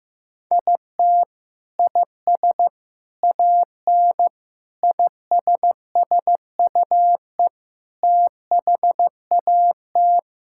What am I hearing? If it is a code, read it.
Morse code, heard as a keyed tone, about 15 wpm, 700 Hz. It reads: IT IS AN ISSUE THAT
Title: Sentences from Top 1000 Words - Review Part 6 - 15wpm